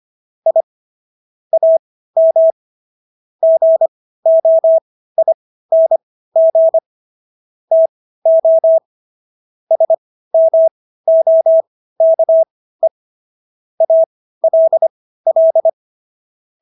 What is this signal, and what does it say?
Morse code, heard as a keyed tone, about 25 wpm, spaced as for 15 wpm, 650 Hz. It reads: I AM GOING TO SMOKE ALL